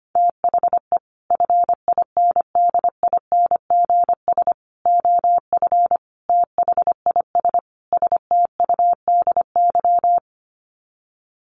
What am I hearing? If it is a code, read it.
Morse code, heard as a keyed tone, about 25 wpm, 700 Hz. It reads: THE FINDINGS OF THIS STUDY
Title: Sets of 5 Words Encore2 25wpm